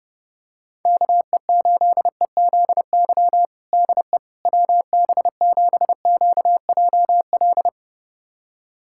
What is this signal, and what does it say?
Morse code, heard as a keyed tone, about 32 wpm, 690 Hz. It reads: KE8EZY DE WB7QJL
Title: USA Callsign DE Callsign - 6 Characters - 30wpm